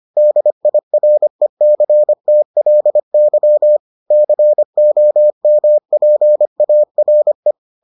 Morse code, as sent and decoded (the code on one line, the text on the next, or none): -.. .. .-. -.-. - .-.. -.-- / -.-. --- -- .--. .- .-. .
DIRECTLY COMPARE